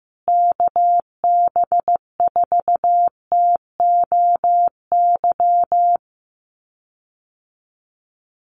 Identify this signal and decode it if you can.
Morse code, heard as a keyed tone, about 15 wpm, 700 Hz. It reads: KB4TOY
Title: USA Call Signs Repeated 2x 15wpm